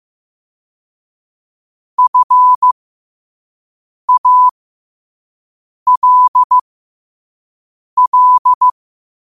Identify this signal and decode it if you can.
Morse code, heard as a keyed tone, about 15 wpm, spaced as for 5 wpm, 1000 Hz. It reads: FALL